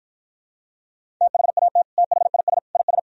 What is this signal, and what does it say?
Morse code, heard as a keyed tone, about 52 wpm, 700 Hz. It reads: THAT THIS IS